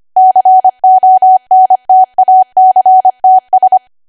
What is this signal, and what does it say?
Morse code, heard as a keyed tone, about 25 wpm, 750 Hz. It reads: CONTACTS
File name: Archive / 25 WPM / 25WPM